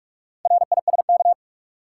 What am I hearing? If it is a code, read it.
Morse code, heard as a keyed tone, about 46 wpm, 700 Hz. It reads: RISK
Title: Sentences from Top 600 Words - New Word Drill - 45wpm